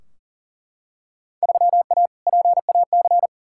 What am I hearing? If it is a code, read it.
Morse code, heard as a keyed tone, about 38 wpm, 700 Hz. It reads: 3A PAC
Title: Field Day Exchanges Encore - 40wpm